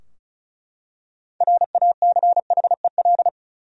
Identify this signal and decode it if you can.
Morse code, heard as a keyed tone, about 36 wpm, 700 Hz. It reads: RACHEL
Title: USA Common Names - 35wpm